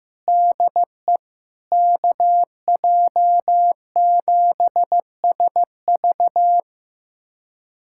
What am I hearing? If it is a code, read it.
Morse code, heard as a keyed tone, about 15 wpm, 700 Hz. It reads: DE KJ7SV
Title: USA Callsign DE Callsign - 5 Characters - 15wpm